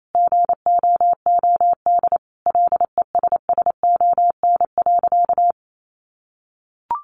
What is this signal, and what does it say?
Morse code, heard as a keyed tone, about 28 wpm, 700 Hz. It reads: GOOD LESSON.
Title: Sentences from Top 1200 Words 28wpm